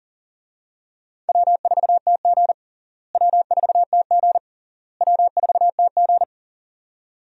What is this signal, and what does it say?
Morse code, heard as a keyed tone, about 40 wpm, 700 Hz. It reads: W4TG W4TG W4TG
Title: USA Call Signs - 4 Characters - Repeated 3x - 40wpm